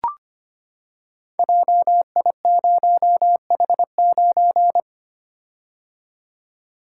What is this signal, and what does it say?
Morse code, heard as a keyed tone, about 25 wpm, 700 Hz. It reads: JI0H9
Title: Code Groups - 5 Letter:Numbers - 25wpm